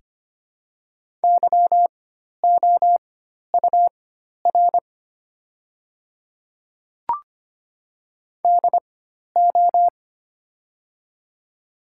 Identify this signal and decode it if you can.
Morse code, heard as a keyed tone, about 25 wpm, spaced as for 12 wpm, 700 Hz. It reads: YOUR DO
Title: 4x Character Spacing - Top 100 Words - 25wpm